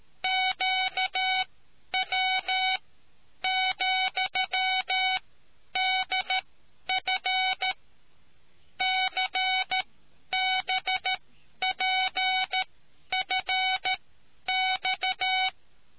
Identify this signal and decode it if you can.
Morse code, heard as a keyed tone, about 13 wpm, spaced as for 10 wpm, 750 Hz. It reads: QW,DF CBPFX